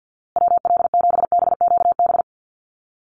Morse code, heard as a keyed tone, about 48 wpm, 700 Hz. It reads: WL7BQB
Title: USA Call Signs Repeated 3x 50wpm